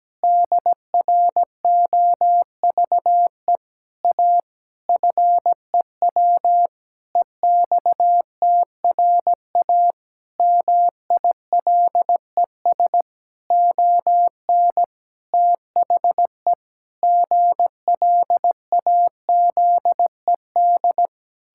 Morse code, heard as a keyed tone, about 17 wpm, 700 Hz. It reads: DROVE A FEW EXTRA MILES ON THE GLAZED
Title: Pangram Challenge v2 - 17wpm